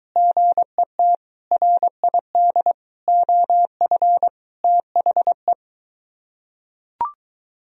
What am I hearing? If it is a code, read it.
Morse code, heard as a keyed tone, about 22 wpm, 700 Hz. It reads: GET RID OF THE